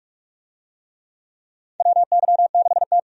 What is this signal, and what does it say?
Morse code, heard as a keyed tone, about 44 wpm, 700 Hz. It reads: WY6T
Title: USA Call Signs - 4 Characters - 45wpm